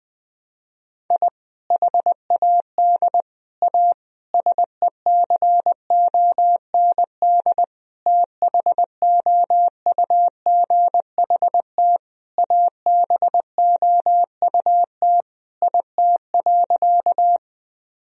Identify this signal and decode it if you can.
Morse code, heard as a keyed tone, about 20 wpm, 700 Hz. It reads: I HAD A SECOND THOUGHT ABOUT IT.
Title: Sentences from Top 1000 Words - Review Part 1 - 20wpm